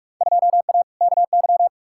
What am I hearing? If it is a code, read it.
Morse code, heard as a keyed tone, about 41 wpm, 700 Hz. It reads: 2A KY
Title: Field Day Exchanges - 45wpm